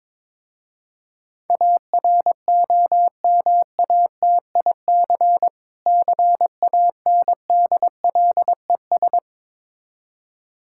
Morse code, heard as a keyed tone, about 22 wpm, 700 Hz. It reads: AROMATIC CANDLES
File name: Difficult Words with Hints - 22wpm